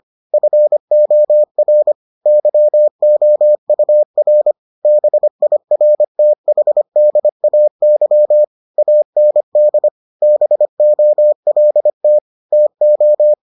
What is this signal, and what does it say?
Morse code, heard as a keyed tone, about 25 wpm, 600 Hz. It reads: FOR YOUR BIRTHDAY AND BOLT TO